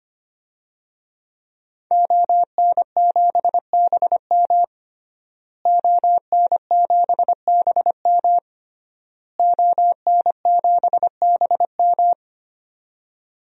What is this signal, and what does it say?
Morse code, heard as a keyed tone, about 25 wpm, 700 Hz. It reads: ON7BM ON7BM ON7BM
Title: DXCC-100 Call Signs - 5 Characters - Repeated 3x - 25wpm